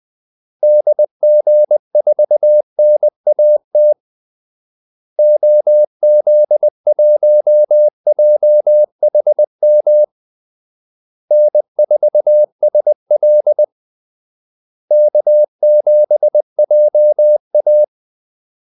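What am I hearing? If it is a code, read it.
Morse code, heard as a keyed tone, about 20 wpm, 600 Hz. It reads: DG4NAT OZ1JHM N4SL K7JA
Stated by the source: Morse call-sign practice at 20 wpm.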